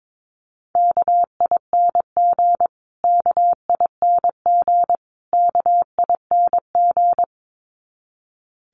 Morse code, heard as a keyed tone, about 22 wpm, 700 Hz. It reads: KING KING KING
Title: Top 500 Words Repeated 22wpm